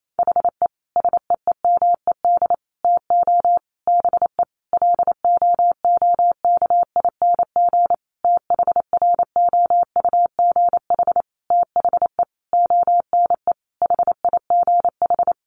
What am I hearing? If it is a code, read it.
Morse code, heard as a keyed tone, about 28 wpm, 700 Hz. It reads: HE SEEMED TO BE LOOKING THROUGH THE ONE HIGH
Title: A Sound Of Thunder 28wpm